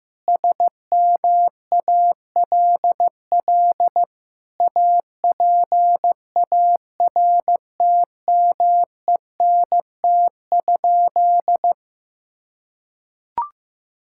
Morse code, heard as a keyed tone, about 15 wpm, 700 Hz. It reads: SMALL APARTMENT?
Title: Sentences from Top 1000 Words - Review Part 6 - 15wpm